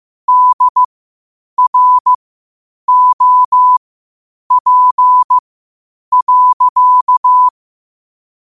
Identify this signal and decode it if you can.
Morse code, heard as a keyed tone, about 15 wpm, spaced as for 8 wpm, 1000 Hz. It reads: DROP.
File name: qotdfiles / 8wpm